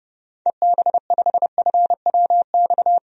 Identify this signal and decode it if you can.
Morse code, heard as a keyed tone, about 30 wpm, 700 Hz. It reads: EB5FWX
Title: DXCC-100 Call Signs - 6 Characters - 30wpm